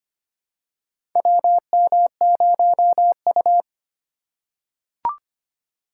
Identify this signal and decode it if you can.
Morse code, heard as a keyed tone, about 25 wpm, 700 Hz. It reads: WM0U
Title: USA Call Signs - 4 Characters - 3x Character Spacing - 25wpm